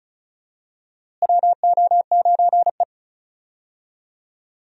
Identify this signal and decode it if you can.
Morse code, heard as a keyed tone, about 34 wpm, 700 Hz. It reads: WO9E
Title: USA Call Signs - 4 Characters - Repeated 2x - 35wpm